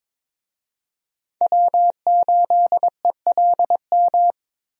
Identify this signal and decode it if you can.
Morse code, heard as a keyed tone, about 22 wpm, 700 Hz. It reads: W8ELM
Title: USA Call Signs - 5 Characters - Repeated 2x - 22wpm